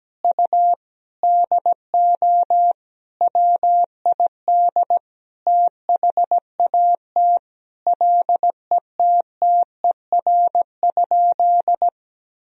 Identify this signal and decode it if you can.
Morse code, heard as a keyed tone, about 17 wpm, 700 Hz. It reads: U DO WID THAT LETTER?
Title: Sentences from Top 300 Words and Abbreviations 17wpm